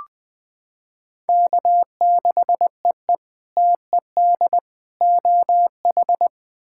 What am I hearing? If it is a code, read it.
Morse code, heard as a keyed tone, about 21 wpm, 700 Hz. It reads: K6EE TED OH
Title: K1USN SST QSO Practice - 20wpm